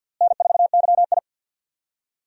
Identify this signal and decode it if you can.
Morse code, heard as a keyed tone, about 46 wpm, 700 Hz. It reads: N4YI
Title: USA Call Signs - 4 Characters - Repeated 2x - 50wpm